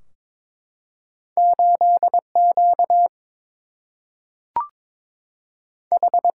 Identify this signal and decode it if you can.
Morse code, heard as a keyed tone, about 22 wpm, 700 Hz. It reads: 8 Q H
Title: Single-Letter Single-Number 22wpm